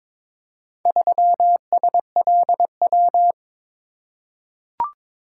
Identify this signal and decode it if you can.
Morse code, heard as a keyed tone, about 22 wpm, 700 Hz. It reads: 3SLW
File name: practice / 22wpm / Four Letter-Number Combinations - 22wpm